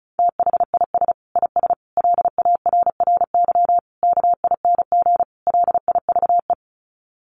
Morse code, heard as a keyed tone, about 34 wpm, 700 Hz. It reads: THIS IS LARRY KING LIVE
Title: Sets of 5 Words Encore2 35wpm